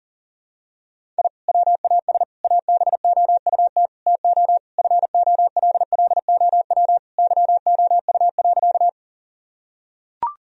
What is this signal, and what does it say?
Morse code, heard as a keyed tone, about 40 wpm, 700 Hz. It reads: I WAS ABOUT TO FOLLOW YOU.
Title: Sentences from Top 1000 Words - Review Part 1 - 40wpm